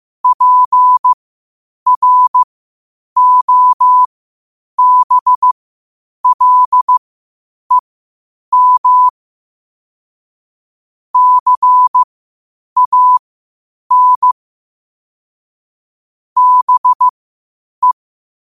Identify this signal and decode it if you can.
Morse code, heard as a keyed tone, about 15 wpm, spaced as for 8 wpm, 1000 Hz. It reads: PROBLEM CAN BE